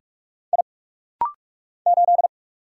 Morse code, heard as a keyed tone, about 48 wpm, 700 Hz. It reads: I 8
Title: Single Letter-Number - Rapid-Fire 45wpm